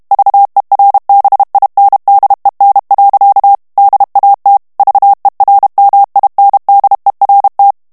Morse code, heard as a keyed tone, about 32 wpm, 800 Hz. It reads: VERBINDEN. DAT VERMINDERT
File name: HOLLANDAIS texte clair 2 vitesse 30 mots minute